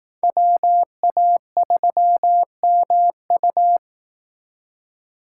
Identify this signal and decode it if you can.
Morse code, heard as a keyed tone, about 18 wpm, 700 Hz. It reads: WA3MU
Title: Speed Racing - USA 2x2 Call Signs 15wpm